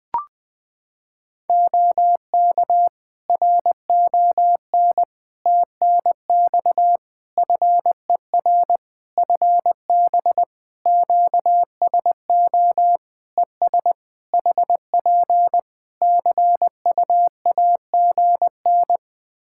Morse code, heard as a keyed tone, about 20 wpm, 700 Hz. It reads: OK RON TNX FER FB QSO ES HP CUAGN
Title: Practice QSO - Exchange 4 - Ending - 20wpm